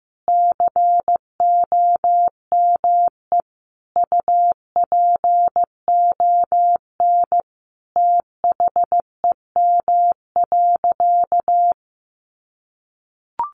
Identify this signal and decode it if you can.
Morse code, heard as a keyed tone, about 15 wpm, 700 Hz. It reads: COME UPON THEM.
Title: Sentences from Top 1200 Words 15wpm